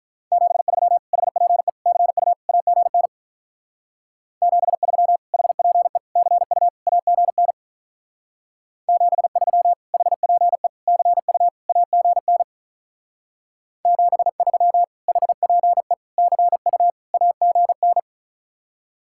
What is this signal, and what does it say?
Morse code, heard as a keyed tone, about 46 wpm, 700 Hz. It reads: 73 HPE CU AGN 73 HPE CU AGN 73 HPE CU AGN 73 HPE CU AGN